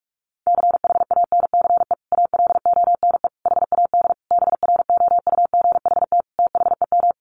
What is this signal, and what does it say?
Morse code, heard as a keyed tone, about 44 wpm, 700 Hz. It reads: CHANCE ALONE HAD BROUGHT THEM
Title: Sentences from Top 1000 Words 45wpm